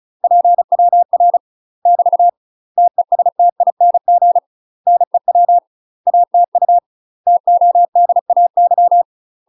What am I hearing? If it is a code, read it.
Morse code, heard as a keyed tone, about 36 wpm, 700 Hz. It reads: PWR <BT> TESTING NEW ATU TODAY
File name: UK MASTER QSOs TRAINING / 35WPM / UK MASTER QSOs 0009